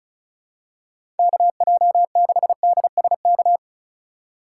Morse code, heard as a keyed tone, about 36 wpm, 700 Hz. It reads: KJ6DSK
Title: USA Call Signs - 6 Characters - 2x Character Spacing - 35wpm